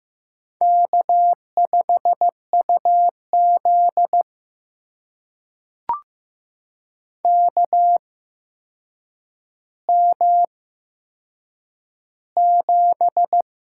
Morse code, heard as a keyed tone, about 15 wpm, spaced as for 7 wpm, 700 Hz. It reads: K5UZ KM7
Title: USA Call Signs - 4 Characters - 8x Character Spacing - 15wpm